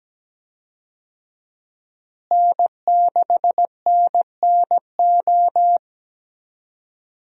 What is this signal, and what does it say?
Morse code, heard as a keyed tone, about 17 wpm, 700 Hz. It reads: N6NNO
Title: USA Call Signs - 5 Characters - Repeated 3x - 17wpm